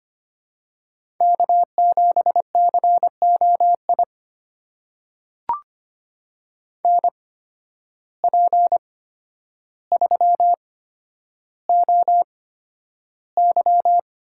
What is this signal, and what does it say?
Morse code, heard as a keyed tone, about 25 wpm, spaced as for 11 wpm, 700 Hz. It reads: K7COI NP3OY
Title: USA Call Signs - 8x Character Spacing - 25wpm